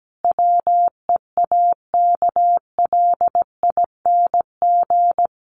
Morse code, heard as a keyed tone, about 17 wpm, 700 Hz. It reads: WEAKLING
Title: A Sound Of Thunder 17wpm